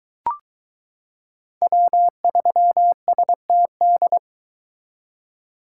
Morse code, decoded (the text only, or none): W3STD